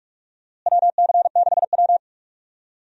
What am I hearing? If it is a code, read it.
Morse code, heard as a keyed tone, about 41 wpm, 700 Hz. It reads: WC6W
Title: USA Call Signs - 4 Characters - Repeated 2x - 45wpm